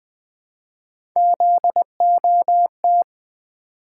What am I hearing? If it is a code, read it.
Morse code, heard as a keyed tone, about 20 wpm, 700 Hz. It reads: ZOT
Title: Three Letter-Number Combinations - 20wpm